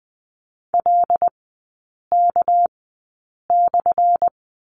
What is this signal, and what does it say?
Morse code, heard as a keyed tone, about 20 wpm, spaced as for 8 wpm, 700 Hz. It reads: LK/
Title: 198 Lesson 32e - Letter k - Three Characters 20wpm